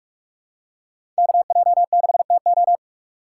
Morse code, heard as a keyed tone, about 46 wpm, 700 Hz. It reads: KJ6TO
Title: USA Call Signs - 5 Characters - 2x Character Spacing - 45wpm